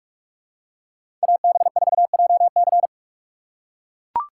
Morse code, heard as a keyed tone, about 41 wpm, 700 Hz. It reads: AB4JC
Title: USA Call Signs - 3x Character Spacing - 45wpm